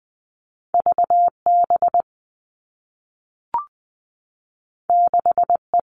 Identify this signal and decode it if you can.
Morse code, heard as a keyed tone, about 20 wpm, 700 Hz. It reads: VB 6E
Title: All Cut Numbers - 2-Digit 20wpm